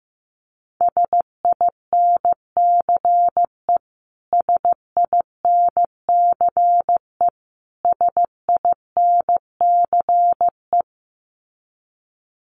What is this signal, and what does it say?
Morse code, heard as a keyed tone, about 15 wpm, 700 Hz. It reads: SINCE SINCE SINCE
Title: Top 500 Words Repeated 15wpm